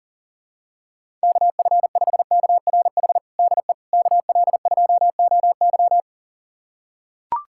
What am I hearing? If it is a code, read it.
Morse code, heard as a keyed tone, about 40 wpm, 700 Hz. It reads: KF5KRH DE KL2OY
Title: USA Callsign DE Callsign 40wpm